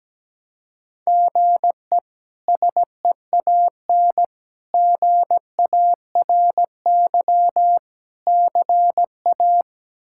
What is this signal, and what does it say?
Morse code, heard as a keyed tone, about 17 wpm, 700 Hz. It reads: GE SEAN GARY CA
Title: K1USN SST QSO Practice - 17wpm